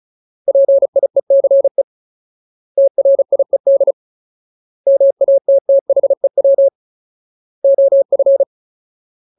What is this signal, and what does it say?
Morse code, heard as a keyed tone, about 35 wpm, 550 Hz. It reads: PIECE TRIED MATTHEW OF
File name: Ultimate Word List / 35WPM Ultimate